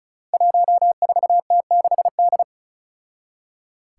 Morse code, heard as a keyed tone, about 36 wpm, 700 Hz. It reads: 14T6D
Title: Five Challenging Letter-Number Combinations 35wpm